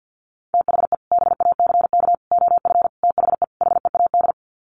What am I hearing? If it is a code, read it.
Morse code, heard as a keyed tone, about 50 wpm, 700 Hz. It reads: THE BACK OF THE HEAD
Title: Sets of 5 Words Encore2 50wpm